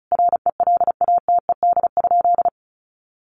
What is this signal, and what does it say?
Morse code, heard as a keyed tone, about 34 wpm, 700 Hz. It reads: RELATED?
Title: Sentences from Top 1400 Words 35wpm